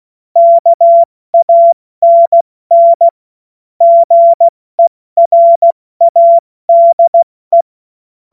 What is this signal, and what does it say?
Morse code, heard as a keyed tone, about 16 wpm, 680 Hz. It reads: KANN GERADE
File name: Morsekurs / Witze / Witz 31. Mai 2022 70 F80